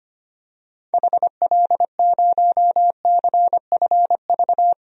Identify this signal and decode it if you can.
Morse code, heard as a keyed tone, about 25 wpm, 700 Hz. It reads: HL0CFV
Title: DXCC-100 Call Signs - 6 Characters - Repeated 2x - 25wpm